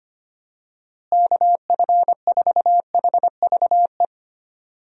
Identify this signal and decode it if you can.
Morse code, heard as a keyed tone, about 25 wpm, 700 Hz. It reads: KF4HVE
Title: USA Call Signs - 6 Characters - Repeated 2x - 25wpm